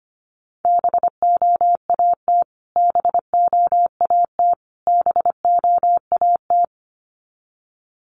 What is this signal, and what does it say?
Morse code, heard as a keyed tone, about 25 wpm, 700 Hz. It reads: BOAT BOAT BOAT
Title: Top 500 Words Repeated 25wpm